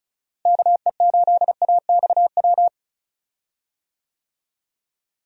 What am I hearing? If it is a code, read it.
Morse code, heard as a keyed tone, about 34 wpm, 700 Hz. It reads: KE8AXW
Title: USA Call Signs - 6 Characters - Repeated 3x - 35wpm